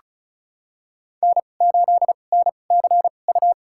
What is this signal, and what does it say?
Morse code, heard as a keyed tone, about 36 wpm, spaced as for 25 wpm, 700 Hz. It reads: N8NCU
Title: USA Call Signs - 5 Characters - 2x Character Spacing - 35wpm